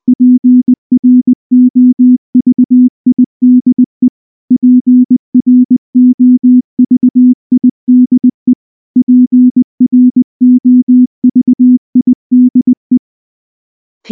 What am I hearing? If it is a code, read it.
Morse code, heard as a keyed tone, about 20 wpm, 250 Hz. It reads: PROVIDE PROVIDE PROVIDE
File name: morse-training-20WPM